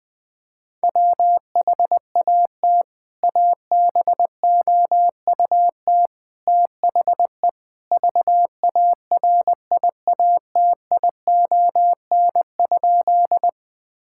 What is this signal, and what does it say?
Morse code, heard as a keyed tone, about 21 wpm, 700 Hz. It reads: WHAT ABOUT THE VARIATION?
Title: Sentences from Top 1600 Words 20wpm